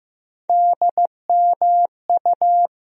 Morse code, heard as a keyed tone, about 15 wpm, 700 Hz. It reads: DMU
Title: USA Call Sign Postfixes - 3 Letters 15wpm